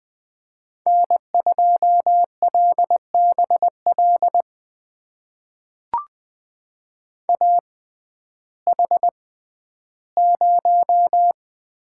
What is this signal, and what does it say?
Morse code, heard as a keyed tone, about 20 wpm, spaced as for 12 wpm, 700 Hz. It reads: N2LBL AH0
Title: USA Call Signs - 6x Character Spacing - 20wpm